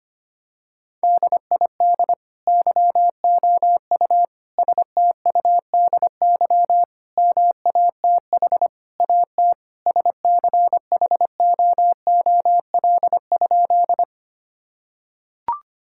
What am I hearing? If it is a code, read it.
Morse code, heard as a keyed tone, about 25 wpm, 700 Hz. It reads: DID YOU STUDY MATH AT SCHOOL?
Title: Sentences from Top 1000 Words - Review Part 6 - 25wpm